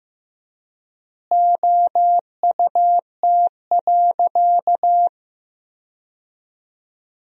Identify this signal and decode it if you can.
Morse code, heard as a keyed tone, about 15 wpm, 700 Hz. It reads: OUT.
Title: 3x Word-Spacing - Sentences from Top 200 Words - 15wpm